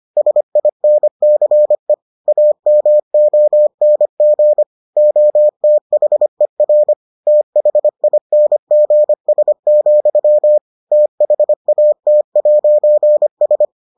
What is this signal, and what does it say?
Morse code, heard as a keyed tone, about 25 wpm, 600 Hz. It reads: SINCE AMONG OTHER THINGS, THAT'S